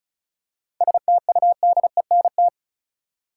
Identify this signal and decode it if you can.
Morse code, heard as a keyed tone, about 34 wpm, 700 Hz. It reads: STUDENT
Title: Seven Letter Words - 35wpm